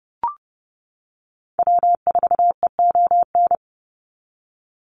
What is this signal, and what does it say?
Morse code, heard as a keyed tone, about 30 wpm, 700 Hz. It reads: W4EON
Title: NA SOTA Activator Callsigns 30wpm